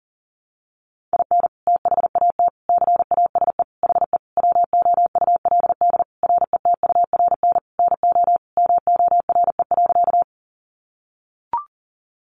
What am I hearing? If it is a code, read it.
Morse code, heard as a keyed tone, about 40 wpm, 700 Hz. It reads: IN THAT CASE HE WOULD RETURN NO MORE.
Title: Sentences from Top 800 Words 40wpm